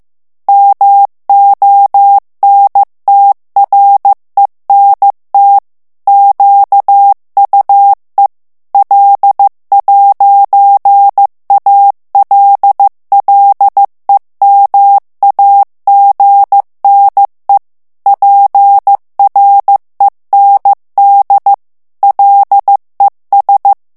Morse code, heard as a keyed tone, about 15 wpm, 800 Hz. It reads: MONTRENT QUE L'ALLEMAGNE PREND LES
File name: morse mp3 / FRANCAIS TEXTE CLAIR 1 VITESSE 14 MOTS MINUTES